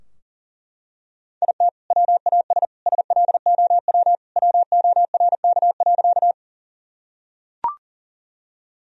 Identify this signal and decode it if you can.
Morse code, heard as a keyed tone, about 40 wpm, 700 Hz. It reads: IT WAS SLOW WORK.
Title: Sentences from Top 1000 Words - Review Part 2 - 40wpm